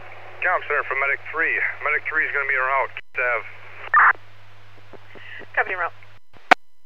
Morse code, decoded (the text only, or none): none